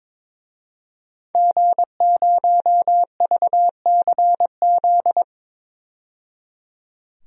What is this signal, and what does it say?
Morse code, heard as a keyed tone, about 22 wpm, 700 Hz. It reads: G0VCZ
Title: DXCC-100 Call Signs - 5 Characters - 22wpm